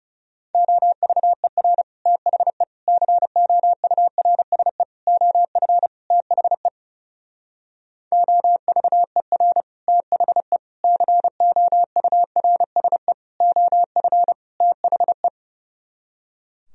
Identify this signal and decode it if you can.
Morse code, heard as a keyed tone, about 34 wpm, 700 Hz. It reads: OVER THE COURSE OF THE OVER THE COURSE OF THE
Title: Speed-Racing - Sets of 5 Words 30wpm